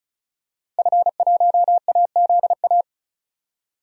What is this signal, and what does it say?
Morse code, heard as a keyed tone, about 36 wpm, 700 Hz. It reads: F1AZA
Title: DXCC-100 Call Signs - 5 Characters - 35wpm